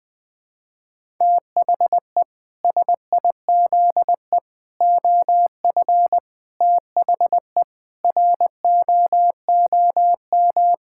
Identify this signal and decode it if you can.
Morse code, heard as a keyed tone, about 20 wpm, 700 Hz. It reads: THE SIZE OF THE ROOM
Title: Sentences with 4-Word Phrases - Spoken Beginning - 20wpm